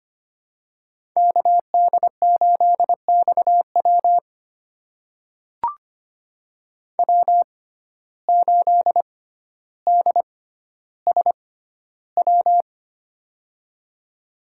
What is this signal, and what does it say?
Morse code, heard as a keyed tone, about 25 wpm, spaced as for 13 wpm, 700 Hz. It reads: KD8XW W8DSW
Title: USA Call Signs - 5 Characters - 6x Character Spacing - 25wpm